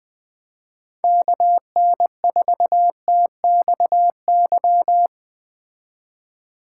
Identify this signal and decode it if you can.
Morse code, heard as a keyed tone, about 19 wpm, 700 Hz. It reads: KN4TXY
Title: Speed Racing - USA 2x3 Call Signs 17wpm